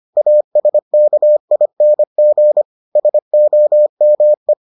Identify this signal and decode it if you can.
Morse code, heard as a keyed tone, about 25 wpm, 600 Hz. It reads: ASKING SOME